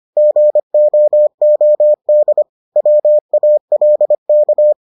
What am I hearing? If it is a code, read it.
Morse code, heard as a keyed tone, about 25 wpm, 600 Hz. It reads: GOOD WALK